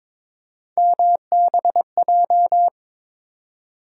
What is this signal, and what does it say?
Morse code, heard as a keyed tone, about 22 wpm, 700 Hz. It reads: MBJ
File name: practice / 22wpm / Three Letter-Number Combinations - 22wpm